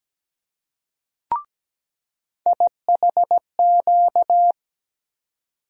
Morse code, heard as a keyed tone, about 17 wpm, 700 Hz. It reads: IHQ
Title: USA Call Sign Postfixes - 3 Letters 17wpm